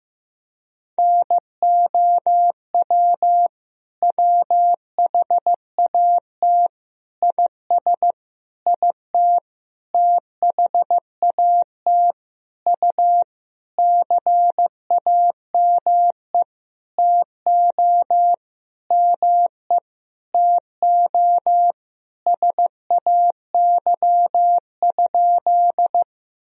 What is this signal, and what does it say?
Morse code, heard as a keyed tone, about 15 wpm, 700 Hz. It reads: NOW WHAT IS IT THAT U CAME TO ME TO SAY?
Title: Sentences from Top 200 Words and Abbreviations 15wpm